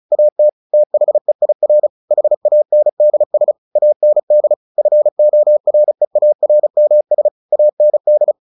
Morse code, heard as a keyed tone, about 34 wpm, 600 Hz. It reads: AT THEIR HANDS AND FOREARMS AND
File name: the old man and the sea 2 35wpm